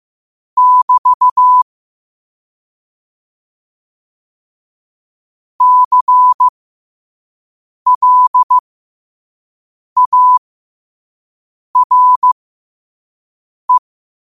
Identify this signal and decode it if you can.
Morse code, heard as a keyed tone, about 15 wpm, spaced as for 5 wpm, 1000 Hz. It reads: <BT> CLARE
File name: qotdfiles / 5wpm